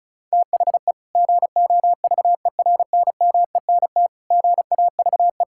Morse code, heard as a keyed tone, about 36 wpm, 690 Hz. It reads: THE GOVERNMENT GAVE
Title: Sentences from Top 1000 Words - Review Part 5 - 35wpm